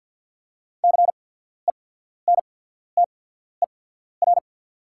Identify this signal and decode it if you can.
Morse code, heard as a keyed tone, about 48 wpm, spaced as for 14 wpm, 690 Hz. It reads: CENTER
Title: 8x Character Spacing - Top 500 Words - 50wpm